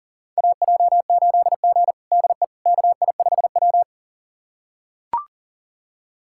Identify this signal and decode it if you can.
Morse code, heard as a keyed tone, about 37 wpm, 700 Hz. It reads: AJ8G DE KI5W
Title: USA Callsign DE Callsign - 4 Characters - 40wpm